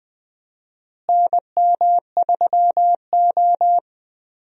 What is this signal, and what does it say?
Morse code, heard as a keyed tone, about 21 wpm, 700 Hz. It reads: NM3O
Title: USA Call Signs - 4 Characters - Repeated 3x - 20wpm